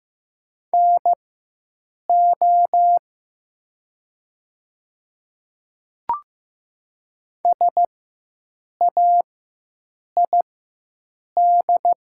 Morse code, heard as a keyed tone, about 15 wpm, spaced as for 7 wpm, 700 Hz. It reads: NO SAID
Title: 4x Character Spacing - Top 500 Words - 15wpm